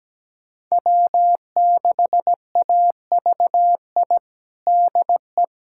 Morse code, heard as a keyed tone, about 17 wpm, 700 Hz. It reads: W6AVI DE
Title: USA Callsign DE Callsign - 5 Characters - 17wpm